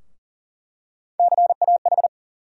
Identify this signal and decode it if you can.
Morse code, heard as a keyed tone, about 40 wpm, 700 Hz. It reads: CAH
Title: USA Call Sign Postfixes - 3 Letters 40wpm